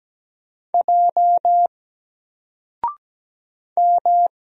Morse code, heard as a keyed tone, about 17 wpm, 700 Hz. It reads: J M
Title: Single Letter-Number 17wpm